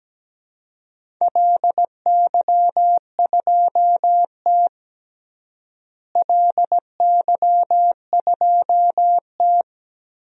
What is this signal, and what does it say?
Morse code, heard as a keyed tone, about 17 wpm, 700 Hz. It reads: LY2T LY2T
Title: DXCC-100 Call Signs - 4 Characters - Repeated 2x - 17wpm